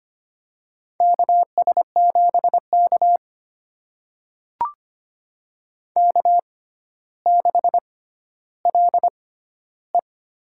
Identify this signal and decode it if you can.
Morse code, heard as a keyed tone, about 25 wpm, spaced as for 13 wpm, 700 Hz. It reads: KS7K K6LE
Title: USA Call Signs - 4 Characters - 6x Character Spacing - 25wpm